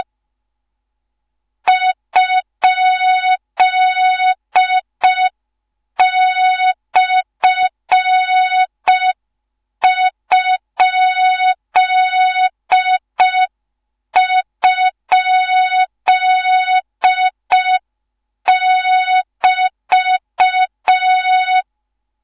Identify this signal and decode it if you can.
Morse code, heard as a keyed tone, about 5 wpm, 3700 Hz. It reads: ?/??<BT>
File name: Morze-HA2MN / hang / 13IRASJELEK1 HA2MN